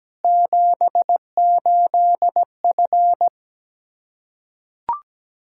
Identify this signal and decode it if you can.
Morse code, heard as a keyed tone, about 17 wpm, 700 Hz. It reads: 78F
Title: Practice QSO - Exchange 2 - Info - 17wpm